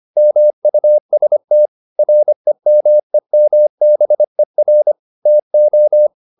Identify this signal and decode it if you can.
Morse code, heard as a keyed tone, about 25 wpm, 600 Hz. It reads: MUST REMEMBER TO
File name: the old man and the sea part 1 25wpm